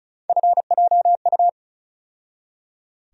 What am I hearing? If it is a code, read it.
Morse code, heard as a keyed tone, about 34 wpm, 700 Hz. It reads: FJU
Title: USA Call Sign Postfixes - 3 Letters 35wpm